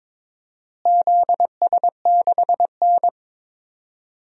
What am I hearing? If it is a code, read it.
Morse code, heard as a keyed tone, about 22 wpm, 700 Hz. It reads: ZS6N
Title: DXCC-100 Call Signs - 4 Characters - Repeated 3x - 22wpm